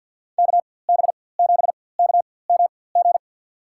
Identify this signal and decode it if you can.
Morse code, heard as a keyed tone, about 51 wpm, spaced as for 23 wpm, 700 Hz. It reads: KB7KMG
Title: USA Call Signs - 6 Characters - 4x Character Spacing - 50wpm